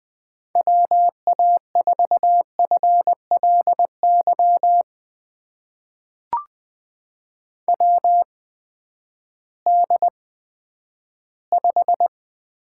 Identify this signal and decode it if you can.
Morse code, heard as a keyed tone, about 20 wpm, spaced as for 11 wpm, 700 Hz. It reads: WA4FLY WD5
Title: USA Call Signs - 6 Characters - 8x Character Spacing - 20wpm